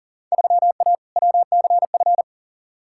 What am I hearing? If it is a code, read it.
Morse code, heard as a keyed tone, about 40 wpm, 700 Hz. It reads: 3A WCF
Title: Field Day Exchanges Encore - 40wpm